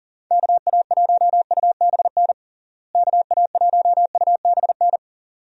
Morse code, heard as a keyed tone, about 40 wpm, 700 Hz. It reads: KA1UBN KA1UBN
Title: USA Call Signs - 6 Characters - Repeated 2x - 40wpm